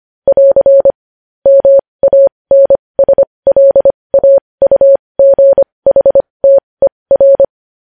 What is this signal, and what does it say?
Morse code, heard as a keyed tone, about 25 wpm, spaced as for 20 wpm, 550 Hz. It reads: <AR> MANSLAUGHTER
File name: podcast / News headlines-1x-25x20